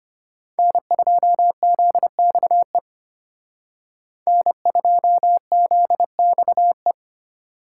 Morse code, heard as a keyed tone, about 32 wpm, 700 Hz. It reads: N2ZXE N2ZXE